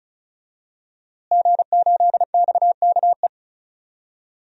Morse code, heard as a keyed tone, about 34 wpm, 700 Hz. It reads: G8XKE
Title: DXCC-100 Call Signs - 5 Characters - Repeated 2x - 35wpm